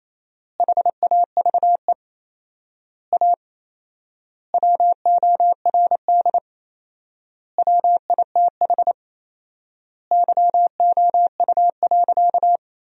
Morse code, heard as a keyed tone, about 28 wpm, 700 Hz. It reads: HAVE A WORD WITH YOU.
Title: 4x Word-Spacing - Sentences from Top 300 Words - 28wpm